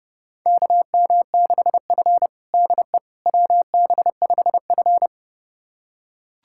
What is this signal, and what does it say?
Morse code, heard as a keyed tone, about 30 wpm, 700 Hz. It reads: KM6F DE WB5F
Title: USA Callsign DE Callsign - 4 Characters - 30wpm